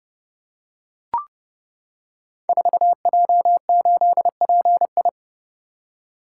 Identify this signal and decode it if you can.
Morse code, heard as a keyed tone, about 31 wpm, 700 Hz. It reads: 4J8PI
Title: Code Groups - 5 Letter:Numbers - 30wpm